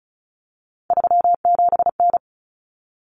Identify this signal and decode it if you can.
Morse code, heard as a keyed tone, about 34 wpm, 700 Hz. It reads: 37N
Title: Common Cut Numbers - 3-Digit 35wpm